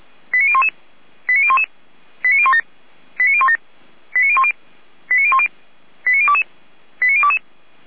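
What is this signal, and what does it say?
Five-tone selective call (ZVEI), heard as quick runs of five tones, tones of about 70 ms. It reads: 89011 89011 89017 89017 89010 89010 89022 89022
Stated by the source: ZVEI five-tone call received by SDR, Feuerwehr Orlamünde.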